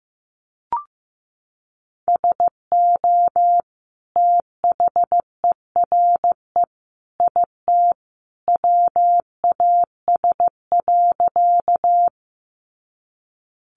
Morse code, heard as a keyed tone, about 15 wpm, 700 Hz. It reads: SO THERE IT WAS.
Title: Sentences from Top 100 Words 15wpm